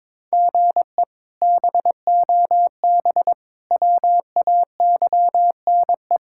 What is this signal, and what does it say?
Morse code, heard as a keyed tone, about 22 wpm, 700 Hz. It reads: GE BOB WAYNE
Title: K1USN SST QSO Practice - 22wpm